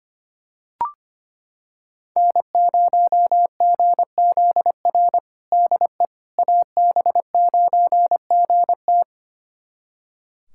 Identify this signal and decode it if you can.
Morse code, heard as a keyed tone, about 25 wpm, 700 Hz. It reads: N0GZR DE AB9GT
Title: USA Callsign DE Callsign - 5 Characters - 25wpm